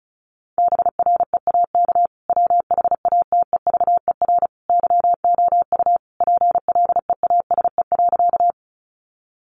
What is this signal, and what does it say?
Morse code, heard as a keyed tone, about 34 wpm, 700 Hz. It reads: BREAK WHATEVER YOU PLEASE.
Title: Sentences from Top 1500 Words 35wpm